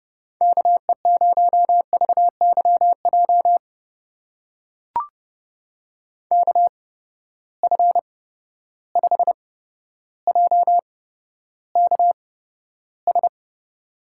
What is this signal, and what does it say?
Morse code, heard as a keyed tone, about 30 wpm, spaced as for 13 wpm, 700 Hz. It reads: KE0VYJ KF5JKS
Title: USA Call Signs - 6 Characters - 8x Character Spacing - 30wpm